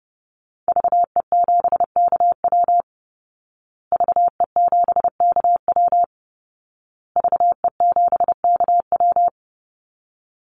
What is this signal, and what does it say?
Morse code, heard as a keyed tone, about 30 wpm, 700 Hz. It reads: VE7KW VE7KW VE7KW
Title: Call Signs Repeated 2x - 30wpm